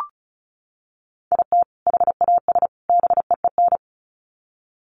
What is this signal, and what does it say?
Morse code, heard as a keyed tone, about 34 wpm, 700 Hz. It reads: IT HAS BEEN
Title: Sets of 3 Words 35wpm